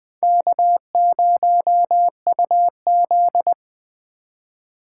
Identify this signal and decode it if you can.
Morse code, heard as a keyed tone, about 21 wpm, 700 Hz. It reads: K0UZ